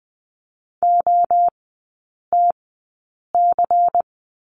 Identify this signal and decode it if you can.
Morse code, heard as a keyed tone, about 20 wpm, spaced as for 8 wpm, 700 Hz. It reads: OTC
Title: 093 Lesson 16e - Letter c - Three Characters 20wpm